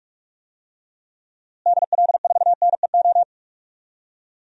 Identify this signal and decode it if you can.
Morse code, heard as a keyed tone, about 42 wpm, 700 Hz. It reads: DL4NEO
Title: DXCC-100 Call Signs - 6 Characters - Repeated 2x - 45wpm